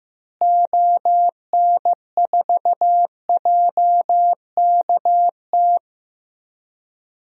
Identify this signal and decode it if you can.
Morse code, heard as a keyed tone, about 15 wpm, 700 Hz. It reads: ON4JKT
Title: DXCC-100 Call Signs - 6 Characters - Repeated 2x - 15wpm